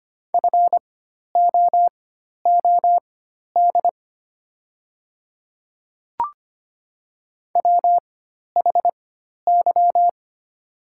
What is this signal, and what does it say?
Morse code, heard as a keyed tone, about 25 wpm, spaced as for 12 wpm, 700 Hz. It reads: FOOD WHY
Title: 4x Character Spacing - Top 300 Words - 25wpm